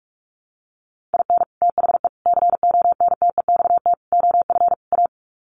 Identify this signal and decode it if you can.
Morse code, heard as a keyed tone, about 44 wpm, 700 Hz. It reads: IN THE CONTEXT OF A
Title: Sets of 5 Words Encore 45wpm